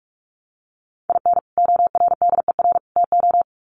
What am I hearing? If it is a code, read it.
Morse code, heard as a keyed tone, about 44 wpm, 700 Hz. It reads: IN ORDER TO
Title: Sets of 3 Words 45wpm